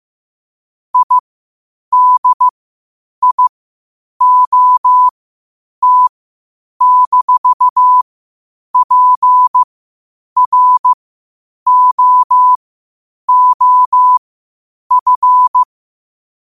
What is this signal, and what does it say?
Morse code, heard as a keyed tone, about 15 wpm, spaced as for 8 wpm, 1000 Hz. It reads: IDIOT-PROOF
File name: qotdfiles / 8wpm